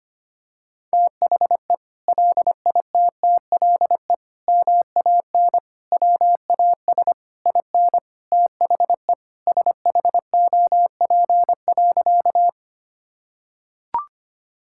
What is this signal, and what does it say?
Morse code, heard as a keyed tone, about 25 wpm, 700 Hz. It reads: THE LITTLE MAN WAS IN THE SHOP.
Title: Sentences from Top 1600 Words 25wpm